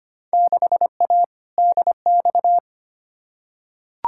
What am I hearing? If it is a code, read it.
Morse code, heard as a keyed tone, about 25 wpm, 700 Hz. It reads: 6A DX
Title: Field Day Exchanges Encore - 25wpm